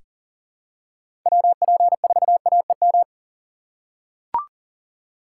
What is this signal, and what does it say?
Morse code, heard as a keyed tone, about 40 wpm, 700 Hz. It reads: WP4AEM
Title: USA Call Signs - 2x Character Spacing - 40wpm